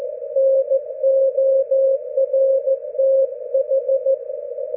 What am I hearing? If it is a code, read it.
Morse code, heard as a keyed tone, about 14 wpm, 540 Hz. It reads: NORTH
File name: GW3UEP / EI0CF P PortlickHarbour 19June12